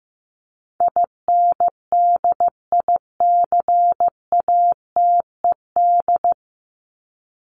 Morse code, heard as a keyed tone, about 15 wpm, 700 Hz. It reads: INDICATED
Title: ED Suffix - Top 100 Words with Base Words 15wpm